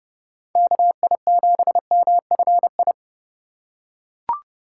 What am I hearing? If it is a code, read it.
Morse code, heard as a keyed tone, about 30 wpm, 700 Hz. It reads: KI7MFI